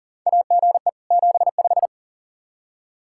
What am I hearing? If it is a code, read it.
Morse code, heard as a keyed tone, about 40 wpm, 700 Hz. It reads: AGE 75
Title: Practice QSO - Exchange 3 - More Info - 40wpm